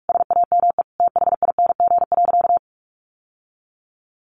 Morse code, heard as a keyed tone, about 44 wpm, 700 Hz. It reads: SAME THING.
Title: Sentences from Top 1400 Words 45wpm